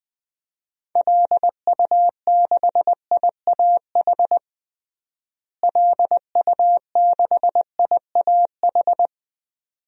Morse code, heard as a keyed tone, about 20 wpm, 700 Hz. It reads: LU6IAH LU6IAH
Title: DXCC-100 Call Signs - 6 Characters - Repeated 2x - 20wpm